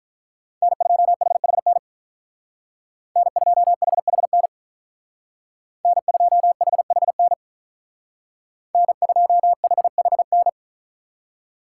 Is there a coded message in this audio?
Yes: Morse code from 0.6 to 10.5 s, heard as a keyed tone, about 45 wpm, 700 Hz. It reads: N2HHN N2HHN N2HHN N2HHN